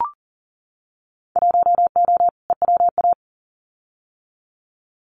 Morse code, heard as a keyed tone, about 38 wpm, 700 Hz. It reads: 1O EWA
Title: Winter Field Day Exchanges Encore 40wpm